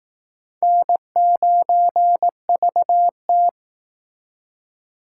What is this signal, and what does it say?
Morse code, heard as a keyed tone, about 18 wpm, 700 Hz. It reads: N9VT
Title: Speed Racing - USA 1x2 Call Signs 15wpm